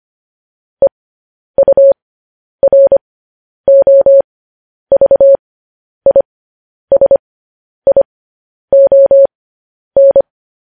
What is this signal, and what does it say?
Morse code, heard as a keyed tone, about 25 wpm, spaced as for 10 wpm, 550 Hz. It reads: EUROVISION
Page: news headlines in morse code at 10 wpm